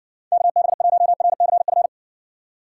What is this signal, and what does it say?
Morse code, heard as a keyed tone, about 56 wpm, 700 Hz. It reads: KB1RCF